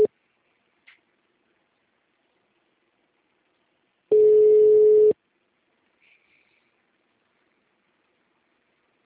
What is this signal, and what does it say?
Ringback tone (Europe, 425 Hz).